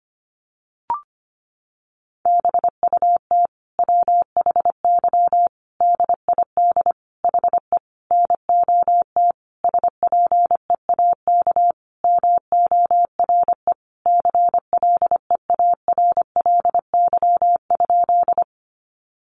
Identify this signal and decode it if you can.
Morse code, heard as a keyed tone, about 25 wpm, 700 Hz. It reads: BUT WHY DID HE NOT SPEAK MORE CLEARLY?
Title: Sentences from Top 1300 Words 25wpm